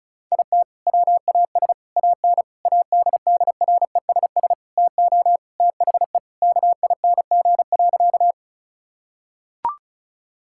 Morse code, heard as a keyed tone, about 36 wpm, 700 Hz. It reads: IT WAS AN ADDRESS TO THE KING.
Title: Sentences from Top 1000 Words - Review Part 6 - 35wpm